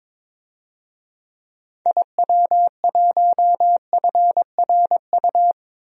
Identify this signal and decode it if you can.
Morse code, heard as a keyed tone, about 22 wpm, 700 Hz. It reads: IW1FRU
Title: DXCC-100 Call Signs - 6 Characters - 22wpm